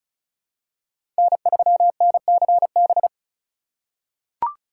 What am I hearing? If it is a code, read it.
Morse code, heard as a keyed tone, about 34 wpm, 700 Hz. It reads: N3NCB
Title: USA Call Signs - 5 Characters - 4x Character Spacing - 35wpm